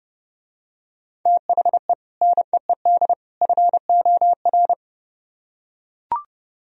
Morse code, heard as a keyed tone, about 30 wpm, 700 Hz. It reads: THE NEED FOR